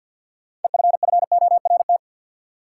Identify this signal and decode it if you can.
Morse code, heard as a keyed tone, about 47 wpm, 700 Hz. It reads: EFFORT
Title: Sets of 4 Words - New Word Drill - 50wpm